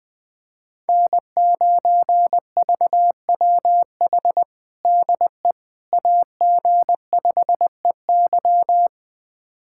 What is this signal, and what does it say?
Morse code, heard as a keyed tone, about 21 wpm, 700 Hz. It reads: N9VWH DE AG5EY
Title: USA Callsign DE Callsign - 5 Characters - 20wpm